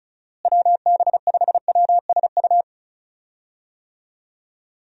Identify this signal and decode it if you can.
Morse code, heard as a keyed tone, about 34 wpm, 700 Hz. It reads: WB5WSU
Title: USA Call Signs - 6 Characters - Repeated 2x - 35wpm